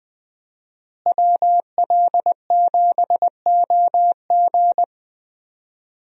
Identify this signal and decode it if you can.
Morse code, heard as a keyed tone, about 21 wpm, 700 Hz. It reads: WL7OG
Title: USA Call Signs - 5 Characters - Repeated 2x - 20wpm